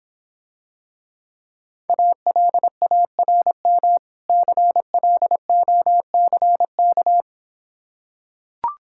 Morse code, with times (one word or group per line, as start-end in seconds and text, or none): 1.9-4.0 ALARM
4.3-7.2 CLOCK